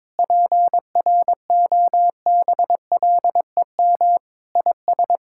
Morse code, heard as a keyed tone, about 21 wpm, 700 Hz. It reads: PROBLEM IS
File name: practice / 22wpm / Sentences with 3-Word Phrases - 22wpm